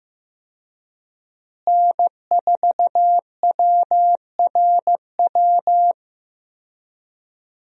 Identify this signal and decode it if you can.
Morse code, heard as a keyed tone, about 15 wpm, 700 Hz. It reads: N4WRW
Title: USA Call Signs - 5 Characters - Repeated 3x - 15wpm